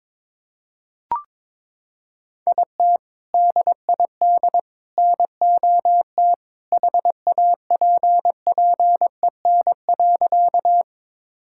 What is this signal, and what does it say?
Morse code, heard as a keyed tone, about 22 wpm, 700 Hz. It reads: IT DID NOT HAPPEN.
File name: practice / 22wpm / Sentences with 3-Word Phrases - 22wpm